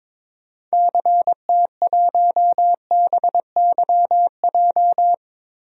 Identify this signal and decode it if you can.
Morse code, heard as a keyed tone, about 22 wpm, 700 Hz. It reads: CT1BYJ
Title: DXCC-100 Call Signs - 6 Characters - 22wpm